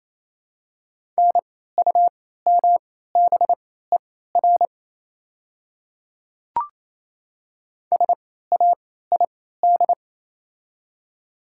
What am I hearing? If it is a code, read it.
Morse code, heard as a keyed tone, about 28 wpm, spaced as for 16 wpm, 700 Hz. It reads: NUMBER SAID
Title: 3x Character Spacing - Top 100 Words - 28wpm